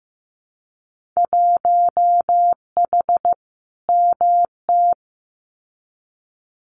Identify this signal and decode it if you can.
Morse code, heard as a keyed tone, about 15 wpm, 700 Hz. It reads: 1H MT
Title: Winter Field Day Exchanges Encore 15wpm